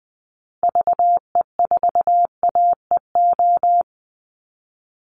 Morse code, heard as a keyed tone, about 20 wpm, 700 Hz. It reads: VE4AEO
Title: Call Signs Repeated 2x - 20wpm